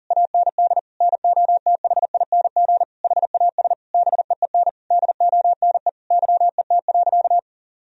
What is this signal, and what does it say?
Morse code, heard as a keyed tone, about 37 wpm, 700 Hz. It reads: AND NOTHING HAS BEEN DONE YET.
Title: Sentences from Top 1000 Words - Review Part 2 - 40wpm